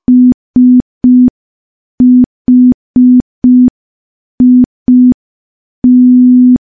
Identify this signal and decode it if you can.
Morse code, heard as a keyed tone, about 5 wpm, 250 Hz. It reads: SHIT
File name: morse / morse-training-05WPM